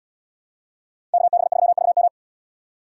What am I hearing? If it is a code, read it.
Morse code, heard as a keyed tone, about 77 wpm, 700 Hz. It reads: KB3VM